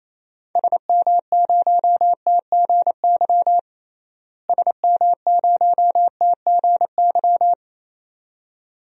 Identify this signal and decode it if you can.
Morse code, heard as a keyed tone, about 28 wpm, 700 Hz. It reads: SM0TGY SM0TGY
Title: DXCC-100 Call Signs - 6 Characters - Repeated 2x - 28wpm